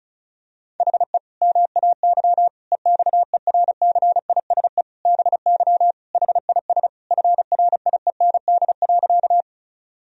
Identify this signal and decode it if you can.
Morse code, heard as a keyed tone, about 36 wpm, 700 Hz. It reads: HE MAY EXERCISE BY HIS FRIEND.
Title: Sentences from Top 1000 Words - Review Part 4 - 35wpm